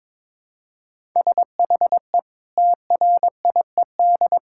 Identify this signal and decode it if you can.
Morse code, heard as a keyed tone, about 22 wpm, 700 Hz. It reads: SHE TRIED
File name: practice / 22wpm / Sentences with 3-Word Phrases - 22wpm